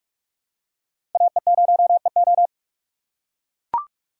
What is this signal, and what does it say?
Morse code, heard as a keyed tone, about 44 wpm, 700 Hz. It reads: AE0EO
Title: USA Call Signs - 5 Characters - 6x Character Spacing - 45wpm